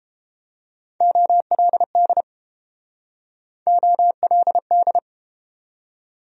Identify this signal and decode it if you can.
Morse code, heard as a keyed tone, about 30 wpm, 700 Hz. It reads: OLD OLD